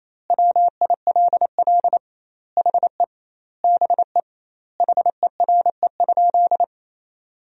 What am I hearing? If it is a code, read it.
Morse code, heard as a keyed tone, about 28 wpm, 700 Hz. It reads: WILL HE BE HERE?
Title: 2x Word-Spacing - Sentences from Top 400 Words - 28wpm